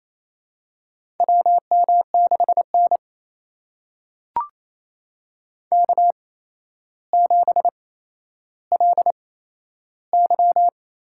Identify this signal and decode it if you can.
Morse code, heard as a keyed tone, about 28 wpm, spaced as for 12 wpm, 700 Hz. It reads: WM6N K7LY
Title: USA Call Signs - 4 Characters - 8x Character Spacing - 28wpm